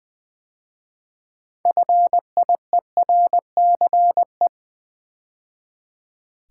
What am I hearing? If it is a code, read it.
Morse code, heard as a keyed tone, about 20 wpm, 700 Hz. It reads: FIERCE
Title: 6 Letter Words - First 3 Letters Given - 20wpm